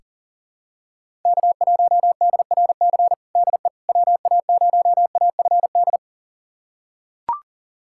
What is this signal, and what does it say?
Morse code, heard as a keyed tone, about 40 wpm, 700 Hz. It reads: K1DRC DE WA0AFD
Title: USA Callsign DE Callsign 40wpm